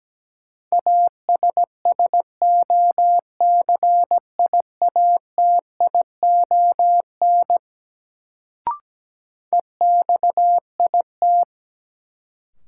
Morse code, heard as a keyed tone, about 17 wpm, 700 Hz. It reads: ASSOCIATION EXIT
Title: Sentences from Top 1100 Words - New Word Drill - 17wpm